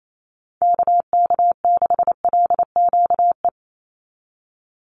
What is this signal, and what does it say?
Morse code, heard as a keyed tone, about 28 wpm, 700 Hz. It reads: KK6LQE
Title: USA Call Signs 28wpm